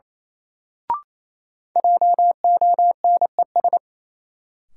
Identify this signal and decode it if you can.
Morse code, heard as a keyed tone, about 27 wpm, 700 Hz. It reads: JONES
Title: Sets of 5 Words - New Word Drill - 28wpm